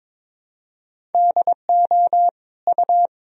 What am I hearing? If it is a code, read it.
Morse code, heard as a keyed tone, about 22 wpm, 700 Hz. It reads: DO U
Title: Sentences from Top 300 Words and Abbreviations 22wpm